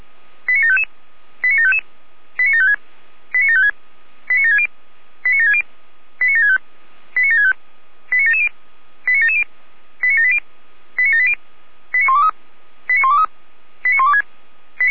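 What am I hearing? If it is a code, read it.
Five-tone selective call (ZVEI), heard as quick runs of five tones, tones of about 70 ms. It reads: 89755 89755 89756 89756 89760 89760 89765 89765 89770 89770 89780 89780 89123 89123 89126
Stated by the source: ZVEI five-tone selective call received by SDR.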